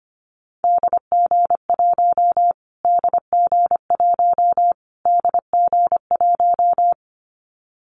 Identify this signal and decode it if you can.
Morse code, heard as a keyed tone, about 25 wpm, 700 Hz. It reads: DG1 DG1 DG1
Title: Callsign Prefixes for Top 25 DXCC - Repeated 3x - 25wpm